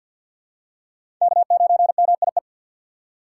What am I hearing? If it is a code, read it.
Morse code, heard as a keyed tone, about 53 wpm, 690 Hz. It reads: K9MIE
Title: USA Call Signs - 2x Character Spacing - 50wpm